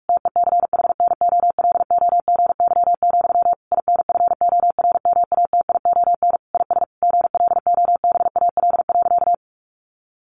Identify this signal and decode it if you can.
Morse code, heard as a keyed tone, about 44 wpm, 700 Hz. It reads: TECHNOLOGY, INFORMATION IS GLOBAL.
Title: Sentences from Top 1200 Words 45wpm